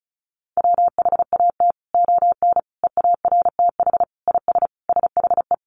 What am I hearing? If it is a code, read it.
Morse code, heard as a keyed tone, about 34 wpm, 700 Hz. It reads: WHAT ON EARTH IS SHE
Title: Sentences from Top 1500 Words 35wpm